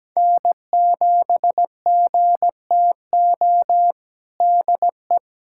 Morse code, heard as a keyed tone, about 17 wpm, 700 Hz. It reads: N7GTO DE
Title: USA Callsign DE Callsign - 5 Characters - 17wpm